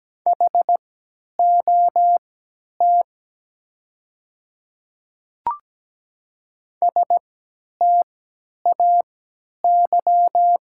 Morse code, heard as a keyed tone, about 17 wpm, spaced as for 10 wpm, 700 Hz. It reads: HOT STAY
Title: 3x Character Spacing - Top 500 Words - 17wpm